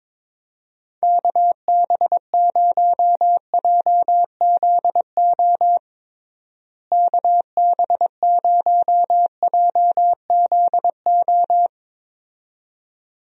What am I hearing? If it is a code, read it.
Morse code, heard as a keyed tone, about 21 wpm, 700 Hz. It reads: KB0JZO KB0JZO
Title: USA Call Signs - 6 Characters - Repeated 2x - 22wpm